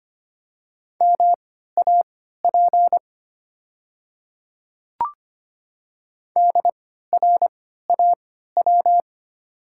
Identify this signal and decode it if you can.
Morse code, heard as a keyed tone, about 25 wpm, spaced as for 14 wpm, 690 Hz. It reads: MAP DRAW
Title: 3x Character Spacing - Top 500 Words - 25wpm